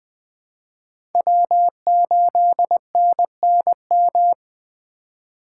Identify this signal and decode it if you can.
Morse code, heard as a keyed tone, about 20 wpm, 700 Hz. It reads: W8NNM
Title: USA Call Signs - 5 Characters - 2x Character Spacing - 20wpm